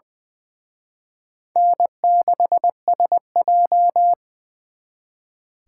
Morse code, heard as a keyed tone, about 23 wpm, 700 Hz. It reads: N6SJ
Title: Speed Racing - USA 1x2 Call Signs 20wpm